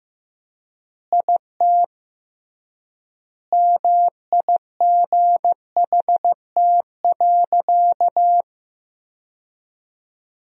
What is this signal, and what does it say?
Morse code, heard as a keyed tone, about 15 wpm, 700 Hz. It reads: IT MIGHT.
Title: 3x Word-Spacing - Sentences from Top 500 Words - 15wpm